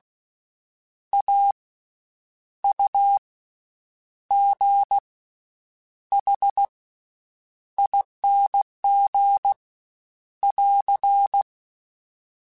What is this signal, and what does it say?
Morse code, heard as a keyed tone, about 16 wpm, spaced as for 8 wpm, 800 Hz. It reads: AUGHING <AR>
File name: Special INGs / Special INGs-mp3s / VerySlowToFaster Ing @15